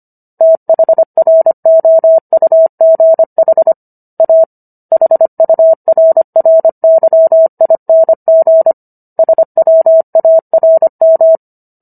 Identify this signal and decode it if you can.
Morse code, heard as a keyed tone, about 25 wpm, 650 Hz. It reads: THROUGH A HURRYING SWARM